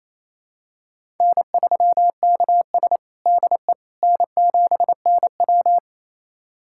Morse code, heard as a keyed tone, about 27 wpm, 690 Hz. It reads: N3KS DE N7NW
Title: USA Callsign DE Callsign - 4 Characters - 28wpm